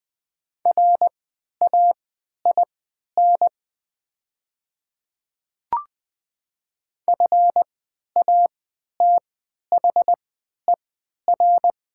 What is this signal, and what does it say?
Morse code, heard as a keyed tone, about 20 wpm, spaced as for 11 wpm, 700 Hz. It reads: RAIN FATHER
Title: 3x Character Spacing - Top 300 Words - 20wpm